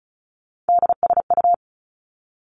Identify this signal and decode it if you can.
Morse code, heard as a keyed tone, about 36 wpm, 700 Hz. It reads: DSU
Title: Three Challenging Letter-Number Combinations 35wpm